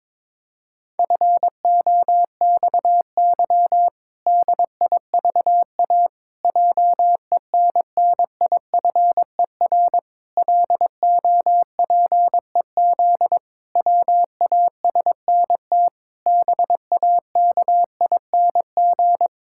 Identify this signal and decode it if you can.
Morse code, heard as a keyed tone, about 22 wpm, 700 Hz. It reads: FOXY DIVA JENNIFER LOPEZ WASNT BAKING
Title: Pangram Challenge v2 - 22wpm